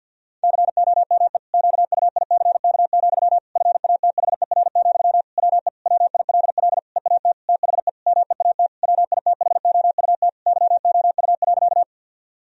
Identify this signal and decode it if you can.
Morse code, heard as a keyed tone, about 51 wpm, 700 Hz. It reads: COME QUICK, FATHER, WE WILL EAT THE MEAT WITHOUT YOU.
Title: Sentences from Top 1000 Words - Review Part 3 - 50wpm